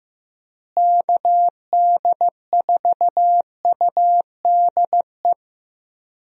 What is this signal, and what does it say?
Morse code, heard as a keyed tone, about 15 wpm, 700 Hz. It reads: KD4UDE
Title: USA Call Signs - 6 Characters - Repeated 3x - 15wpm